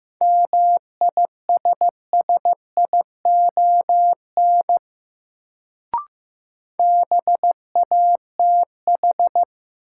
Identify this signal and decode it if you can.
Morse code, heard as a keyed tone, about 15 wpm, 700 Hz. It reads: MISSION BATH
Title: Sentences from Top 1600 Words - New Word Drill - 15wpm